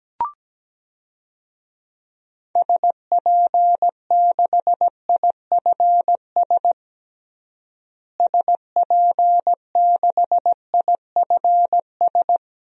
Morse code, heard as a keyed tone, about 17 wpm, 700 Hz. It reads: SP6IFS SP6IFS
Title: DXCC-100 Call Signs - 6 Characters - Repeated 2x - 17wpm